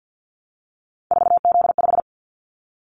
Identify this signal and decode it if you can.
Morse code, heard as a keyed tone, about 50 wpm, 700 Hz. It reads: 4Z5
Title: Callsign Prefixes for Top 50 DXCC - 50wpm